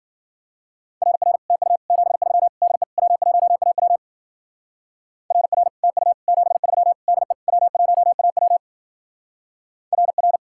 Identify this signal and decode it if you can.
Morse code, heard as a keyed tone, about 49 wpm, 700 Hz. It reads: RR TU 73 DE W1AW RR TU 73 DE W1AW RR
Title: Speed-Racing Common QSO Phrases - 40wpm